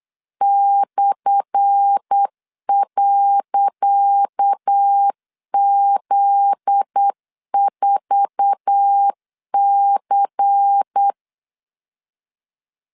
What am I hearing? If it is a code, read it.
Morse code, heard as a keyed tone, about 8 wpm, 800 Hz. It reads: /.Z4C
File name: LEZIONI / 10 SIGNS / 10C DOT 8wpm 3649